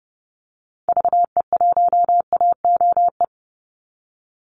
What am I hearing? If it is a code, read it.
Morse code, heard as a keyed tone, about 30 wpm, 700 Hz. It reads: VE1AOE
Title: Call Signs - 30wpm